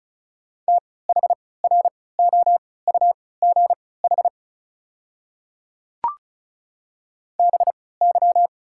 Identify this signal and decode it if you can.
Morse code, heard as a keyed tone, about 36 wpm, spaced as for 20 wpm, 700 Hz. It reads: THROUGH BY
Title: 3x Character Spacing - Top 300 Words - 35wpm